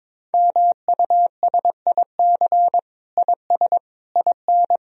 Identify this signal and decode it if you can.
Morse code, heard as a keyed tone, about 22 wpm, 700 Hz. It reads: MUSIC IS IN
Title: Sentences from Top 1000 Words - Review Part 1 - 22wpm